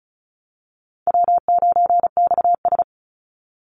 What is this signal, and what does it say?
Morse code, heard as a keyed tone, about 34 wpm, 700 Hz. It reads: W9XS
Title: Call Signs Repeated 2x - 35wpm